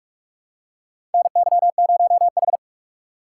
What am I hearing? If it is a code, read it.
Morse code, heard as a keyed tone, about 41 wpm, 700 Hz. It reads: NY0H
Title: USA Call Signs - 4 Characters - 2x Character Spacing - 45wpm